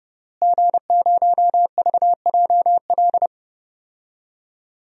Morse code, heard as a keyed tone, about 30 wpm, 700 Hz. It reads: G0VJL
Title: DXCC-100 Call Signs - 5 Characters - Repeated 3x - 30wpm